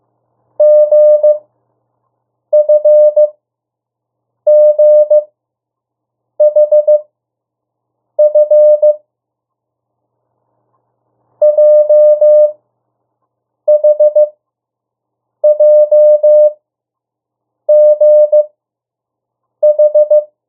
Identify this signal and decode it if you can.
Morse code, heard as a keyed tone, about 15 wpm, spaced as for 6 wpm, 600 Hz. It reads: GFGHF JHJGH
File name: urok 1